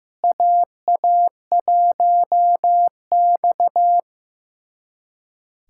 Morse code, heard as a keyed tone, about 15 wpm, 700 Hz. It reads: AA1X
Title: USA Callsign DE Callsign - 4 Characters - 15wpm